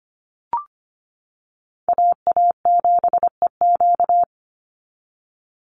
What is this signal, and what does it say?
Morse code, heard as a keyed tone, about 25 wpm, 700 Hz. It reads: AA7EQ
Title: NA SOTA Chaser Callsigns 25wpm